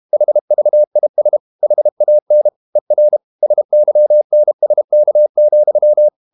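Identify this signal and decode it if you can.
Morse code, heard as a keyed tone, about 32 wpm, 600 Hz. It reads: HVIS HAN ER SYNSK,
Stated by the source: High-speed QRQ CW at 32 WPM.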